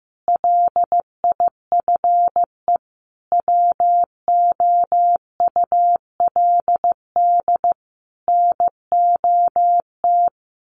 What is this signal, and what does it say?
Morse code, heard as a keyed tone, about 15 wpm, 700 Hz. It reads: LIFE WOULD NOT
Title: Sentences from Top 1300 Words 15wpm